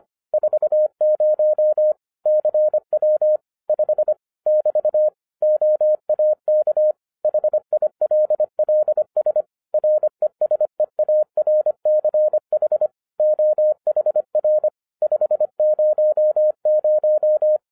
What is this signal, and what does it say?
Morse code, heard as a keyed tone, about 25 wpm, 610 Hz. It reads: 40 CW 5 <BT> OAK HILLS RESEARCH OHR 500
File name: 2011-07-mp3 / n24 QRP OTAKU 25wpm